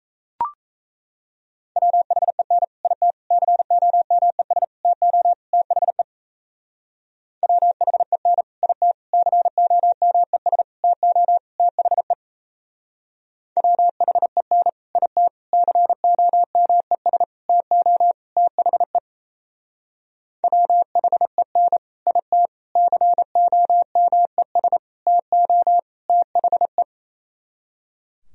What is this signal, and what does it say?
Morse code, heard as a keyed tone, about 34 wpm, 700 Hz. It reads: WHEN IT COMES TO THE WHEN IT COMES TO THE WHEN IT COMES TO THE WHEN IT COMES TO THE
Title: Speed-Racing - Sets of 5 Words 28wpm